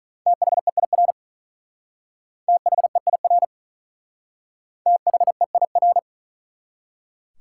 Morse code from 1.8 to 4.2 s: THEIR

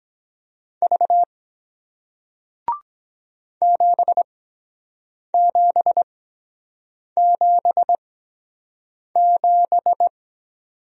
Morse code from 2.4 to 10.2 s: 7777

